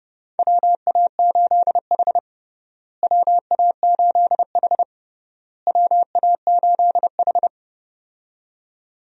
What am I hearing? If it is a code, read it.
Morse code, heard as a keyed tone, about 30 wpm, 700 Hz. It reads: WA8H WA8H WA8H
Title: USA Call Signs - 4 Characters - Repeated 3x - 30wpm